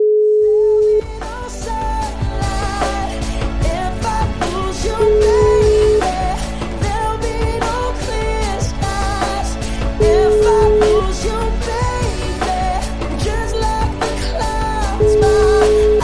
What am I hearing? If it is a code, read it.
Ringback tone (Europe, 425 Hz), with noise.